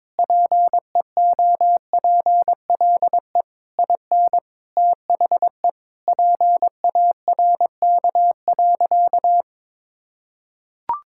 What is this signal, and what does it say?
Morse code, heard as a keyed tone, about 22 wpm, 700 Hz. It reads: PEOPLE IN THE PARK.
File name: practice / 22wpm / Sentences with 3-Word Phrases - 22wpm